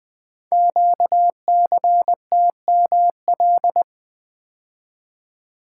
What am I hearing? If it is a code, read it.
Morse code, heard as a keyed tone, about 20 wpm, 700 Hz. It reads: QCTML
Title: Code Groups - 5 Letters - 20wpm